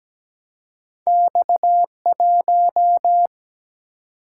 Morse code, heard as a keyed tone, about 17 wpm, 700 Hz. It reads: X1
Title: Single-Letter Single-Number 17wpm